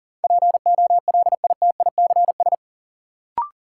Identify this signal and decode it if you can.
Morse code, heard as a keyed tone, about 40 wpm, 700 Hz. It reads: POLITICS
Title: Sentences from Top 1000 Words - New Word Drill - 40wpm